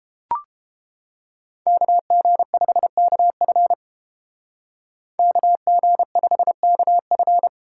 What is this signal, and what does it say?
Morse code, heard as a keyed tone, about 30 wpm, 700 Hz. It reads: KG5KF KG5KF